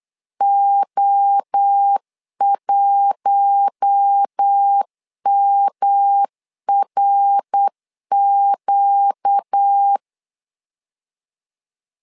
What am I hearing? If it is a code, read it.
Morse code, heard as a keyed tone, about 8 wpm, 800 Hz. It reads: O1MRQ
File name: LEZIONI / 4 Q1 8wpm 3035